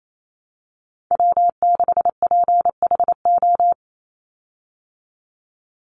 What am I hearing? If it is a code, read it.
Morse code, heard as a keyed tone, about 28 wpm, 700 Hz. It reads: W6PHO
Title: Call Signs Repeated 2x - 28wpm